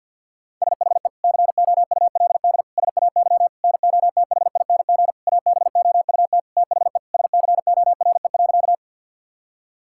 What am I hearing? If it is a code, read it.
Morse code, heard as a keyed tone, about 50 wpm, 700 Hz. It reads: SHE COULD SAY NOTHING ABOUT THE SCORE.
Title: Sentences from Top 1600 Words 50wpm